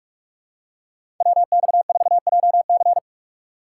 Morse code, heard as a keyed tone, about 43 wpm, 700 Hz. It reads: WX4JC
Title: USA Call Signs - 5 Characters - 4x Character Spacing - 45wpm